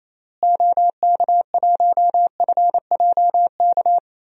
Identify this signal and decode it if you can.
Morse code, heard as a keyed tone, about 28 wpm, 700 Hz. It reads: OK1FJK
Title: DXCC-100 Call Signs - 6 Characters - Repeated 3x - 28wpm